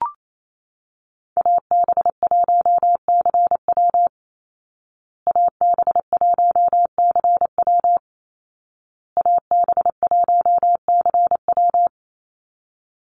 Morse code, heard as a keyed tone, about 28 wpm, 700 Hz. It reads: AB1CW AB1CW AB1CW
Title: Call Signs Repeated 2x - 28wpm